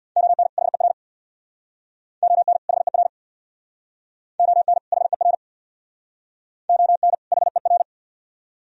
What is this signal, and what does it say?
Morse code, heard as a keyed tone, about 61 wpm, 690 Hz. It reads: ON HER ON HER ON HER ON HER
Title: Speed-Racing - Sets of 2 Words 50wpm